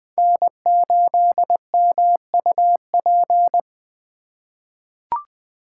Morse code, heard as a keyed tone, about 20 wpm, 700 Hz. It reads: N8MUP
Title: USA Call Signs - 5 Characters - 3x Character Spacing - 20wpm